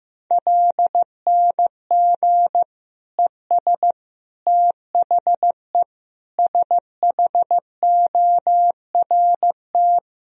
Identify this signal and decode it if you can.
Morse code, heard as a keyed tone, about 15 wpm, 700 Hz. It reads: LNG ES THE SHORT
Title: Sentences from Top 400 Words and Abbreviations 15wpm